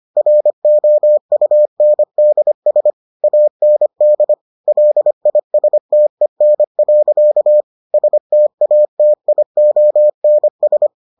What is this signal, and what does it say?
Morse code, heard as a keyed tone, about 25 wpm, 600 Hz. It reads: ROUNDS AND LISTEN. STATIONS